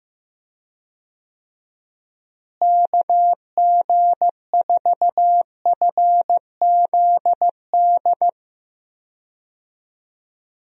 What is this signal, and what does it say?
Morse code, heard as a keyed tone, about 15 wpm, 700 Hz. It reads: KG4FZD
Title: USA Call Signs - 6 Characters - Repeated 2x - 15wpm